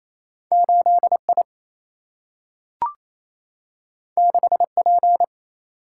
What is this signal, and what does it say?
Morse code, heard as a keyed tone, about 28 wpm, 700 Hz. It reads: 8 I 6 P
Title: Single-Letter Single-Number 28wpm